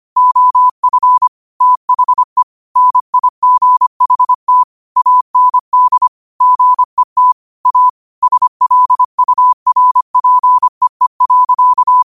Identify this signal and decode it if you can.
Morse code, heard as a keyed tone, about 25 wpm, 1000 Hz. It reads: OF THE NIGHT AND GET A SLURPEE.
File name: qotdfiles / 25wpm